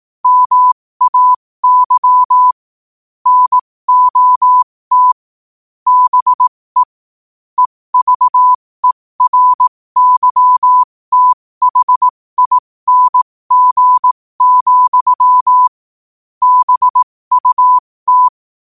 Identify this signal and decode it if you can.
Morse code, heard as a keyed tone, about 18 wpm, spaced as for 15 wpm, 1000 Hz. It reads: MAY NOT BE EVERYTHING, BUT
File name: 15wpm